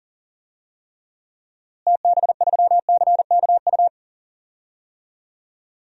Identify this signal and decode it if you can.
Morse code, heard as a keyed tone, about 37 wpm, 700 Hz. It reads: TB3CKU
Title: DXCC-100 Call Signs - 6 Characters - Repeated 2x - 40wpm